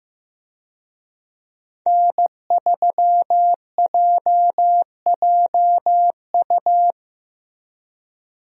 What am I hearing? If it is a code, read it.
Morse code, heard as a keyed tone, about 15 wpm, 700 Hz. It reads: N3JJU
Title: USA Call Signs - 5 Characters - Repeated 2x - 15wpm